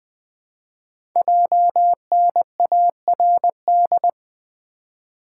Morse code, heard as a keyed tone, about 20 wpm, 700 Hz. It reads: JNARD
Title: Code Groups - 5 Letters - 20wpm